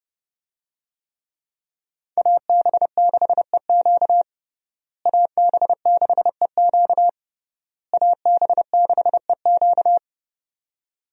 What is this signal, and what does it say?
Morse code, heard as a keyed tone, about 32 wpm, 700 Hz. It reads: AB6EQ AB6EQ AB6EQ
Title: USA Call Signs - 5 Characters - Repeated 3x - 30wpm